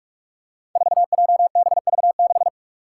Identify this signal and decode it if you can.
Morse code, heard as a keyed tone, about 44 wpm, 700 Hz. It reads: 4JBV6
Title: Five Challenging Letter-Number Combinations 45wpm